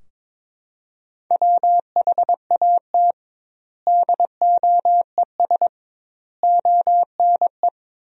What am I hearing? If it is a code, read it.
Morse code, heard as a keyed tone, about 22 wpm, 700 Hz. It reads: WHAT DOES ONE
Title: 2x Word-Spacing - Sentences from Top 200 Words - 22wpm